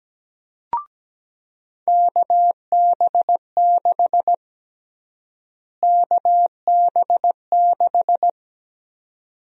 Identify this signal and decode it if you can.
Morse code, heard as a keyed tone, about 17 wpm, 700 Hz. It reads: KB6 KB6
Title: USA Call Sign Prefixes - Repeated 2x - 17wpm